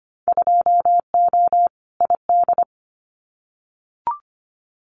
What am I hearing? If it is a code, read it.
Morse code, heard as a keyed tone, about 25 wpm, 700 Hz. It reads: 2O ID
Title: Winter Field Day Exchanges 25wpm